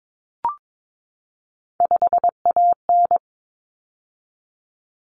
Morse code, heard as a keyed tone, about 22 wpm, 700 Hz. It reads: 5AN
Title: Common Cut Numbers - 3-Digit 22wpm